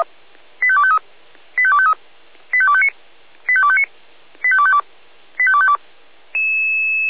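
Five-tone selective call (ZVEI), heard as quick runs of five tones, tones of about 70 ms. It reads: 85253 85253 85259 85259 85252 85252
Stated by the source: Received by SDR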